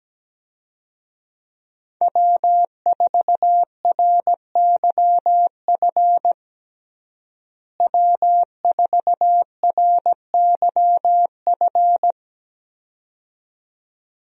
Morse code, heard as a keyed tone, about 17 wpm, 700 Hz. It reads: W4RYF W4RYF
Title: USA Call Signs - 5 Characters - Repeated 2x - 17wpm